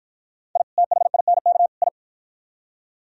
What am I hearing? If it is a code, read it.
Morse code, heard as a keyed tone, about 52 wpm, 700 Hz. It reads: I THINK I